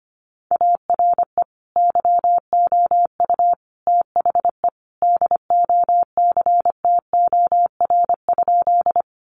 Morse code, heard as a keyed tone, about 25 wpm, 700 Hz. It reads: ARE YOU THE DOCTOR?
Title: Sentences from Top 1500 Words 25wpm